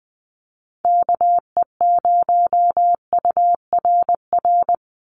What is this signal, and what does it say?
Morse code, heard as a keyed tone, about 20 wpm, 700 Hz. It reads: KE0URR
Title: USA Call Signs 20wpm